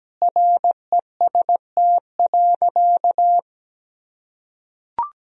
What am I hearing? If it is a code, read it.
Morse code, heard as a keyed tone, about 17 wpm, 700 Hz. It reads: REST.
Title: Sentences from Top 1000 Words - Review Part 4 - 17wpm